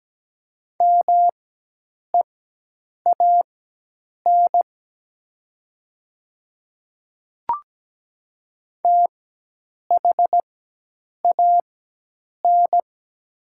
Morse code, heard as a keyed tone, about 17 wpm, spaced as for 8 wpm, 700 Hz. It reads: MEAN THAN
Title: 4x Character Spacing - Top 300 Words - 17wpm